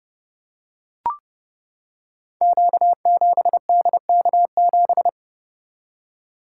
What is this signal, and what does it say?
Morse code, heard as a keyed tone, about 28 wpm, 700 Hz. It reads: Q7DK7
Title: Code Groups - 5 Letter:Numbers - 30wpm